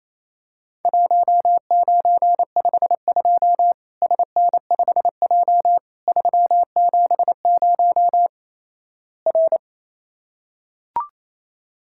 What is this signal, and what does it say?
Morse code, heard as a keyed tone, about 28 wpm, 700 Hz. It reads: 1952 SN5J 370 R
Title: Worked All Europe DX Contest - QTC Exchange - 28wpm